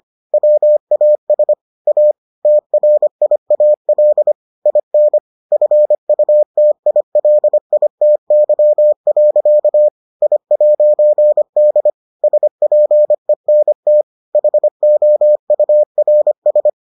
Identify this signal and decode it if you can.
Morse code, heard as a keyed tone, about 25 wpm, 600 Hz. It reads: WAS A TRIAL IN FUTILITY. I'D SPENT HOURS